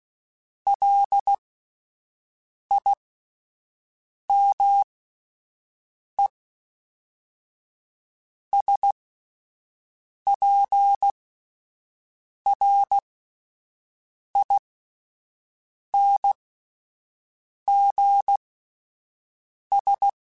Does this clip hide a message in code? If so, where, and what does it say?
Yes: Morse code from 0.7 to 20.1 s, heard as a keyed tone, about 16 wpm, spaced as for 5 wpm, 800 Hz. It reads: LIMESPRINGS